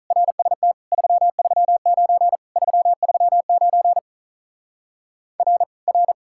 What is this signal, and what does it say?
Morse code, heard as a keyed tone, about 43 wpm, 700 Hz. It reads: RST 339 339 R R
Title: Speed-Racing Common QSO Phrases Encore - 35wpm